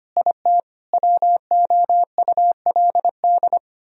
Morse code, heard as a keyed tone, about 29 wpm, 700 Hz. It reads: IT WOULD
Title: Speed-Racing - Sets of 5 Words 25wpm